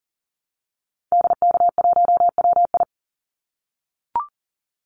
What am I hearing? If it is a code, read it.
Morse code, heard as a keyed tone, about 41 wpm, 700 Hz. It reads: DK1WI
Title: Call Signs - 40wpm